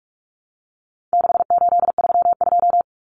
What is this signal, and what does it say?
Morse code, heard as a keyed tone, about 44 wpm, 700 Hz. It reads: 6832
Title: Numbers - 4-Digit 45wpm